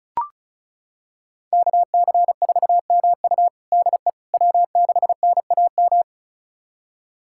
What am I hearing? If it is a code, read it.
Morse code, heard as a keyed tone, about 36 wpm, 700 Hz. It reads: KC4MU DE W6NAM
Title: USA Callsign DE Callsign - 5 Characters - 35wpm